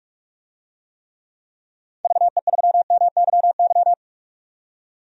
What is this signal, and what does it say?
Morse code, heard as a keyed tone, about 44 wpm, 700 Hz. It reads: VE3MYY
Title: DXCC-100 Call Signs - 6 Characters - 45wpm